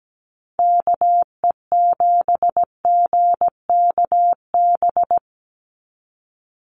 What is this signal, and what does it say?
Morse code, heard as a keyed tone, about 17 wpm, 700 Hz. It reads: KE7GKB
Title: USA Call Signs 17wpm